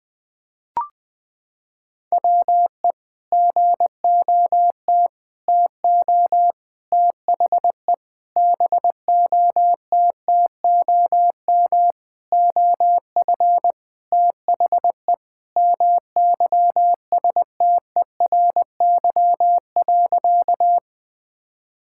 Morse code, heard as a keyed tone, about 20 wpm, 700 Hz. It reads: WE GOT TO THE BOTTOM OF THE MYSTERY.
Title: Sentences with 5-Word Phrases - 20wpm